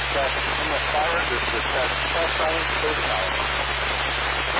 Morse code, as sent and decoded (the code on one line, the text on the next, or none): none